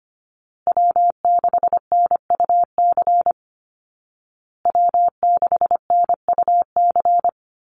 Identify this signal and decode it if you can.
Morse code, heard as a keyed tone, about 25 wpm, 700 Hz. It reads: W6NUC W6NUC
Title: Call Signs Repeated 2x - 25wpm